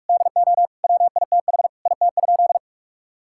Morse code, heard as a keyed tone, about 46 wpm, 700 Hz. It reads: DO WITH IT?
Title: Sentences from Top 1600 Words 45wpm